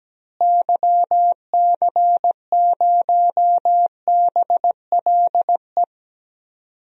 Morse code, heard as a keyed tone, about 17 wpm, 700 Hz. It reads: YC0BLE
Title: DXCC-100 Call Signs - 6 Characters - Repeated 3x - 17wpm